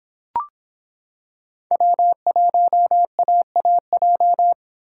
Morse code, heard as a keyed tone, about 26 wpm, 700 Hz. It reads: W1AAJ